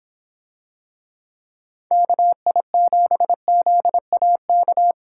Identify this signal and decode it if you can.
Morse code, heard as a keyed tone, about 26 wpm, 700 Hz. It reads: KI7ZAK